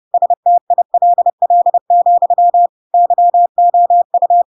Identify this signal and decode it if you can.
Morse code, heard as a keyed tone, about 30 wpm, 700 Hz. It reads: STILL, YOU
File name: Pinocchio / Pinocchio ch4 30 30 ik1hge